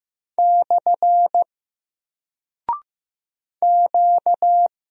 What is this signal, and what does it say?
Morse code, heard as a keyed tone, about 15 wpm, 700 Hz. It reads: / Q
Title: Single Character (Letter:Number:Punctuation:Prosign) 15wpm